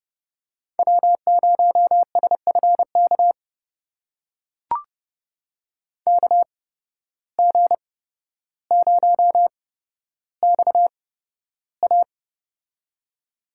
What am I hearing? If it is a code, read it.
Morse code, heard as a keyed tone, about 31 wpm, spaced as for 13 wpm, 700 Hz. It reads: W0SFK KG0XA
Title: USA Call Signs - 5 Characters - 8x Character Spacing - 30wpm